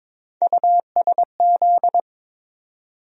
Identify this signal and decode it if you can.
Morse code, heard as a keyed tone, about 22 wpm, 700 Hz. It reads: USZ
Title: USA Call Sign Postfixes - 3 Letters 22wpm